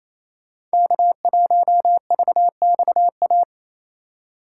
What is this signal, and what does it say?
Morse code, heard as a keyed tone, about 37 wpm, 700 Hz. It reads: K1VXA